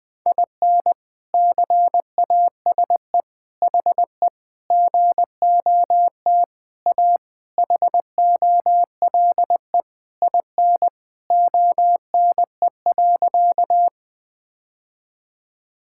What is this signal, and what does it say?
Morse code, heard as a keyed tone, about 20 wpm, 700 Hz. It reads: IN CASE HE GOT A HOLE IN ONE.
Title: All-Age Appropriate Jokes - 20wpm